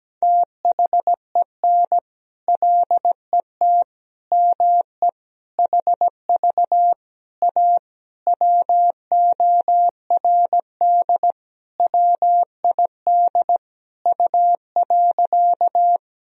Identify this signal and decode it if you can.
Morse code, heard as a keyed tone, about 17 wpm, 700 Hz. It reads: THEN LET ME HV A WORD WID U.
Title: Sentences from Top 300 Words and Abbreviations 17wpm